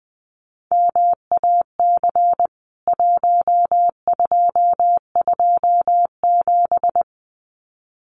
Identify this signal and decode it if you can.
Morse code, heard as a keyed tone, about 20 wpm, 700 Hz. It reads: MAC 1227
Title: CWT Exchange 20wpm